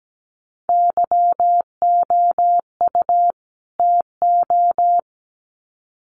Morse code, heard as a keyed tone, about 17 wpm, 700 Hz. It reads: YOU TO
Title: Sets of 2 Words 17wpm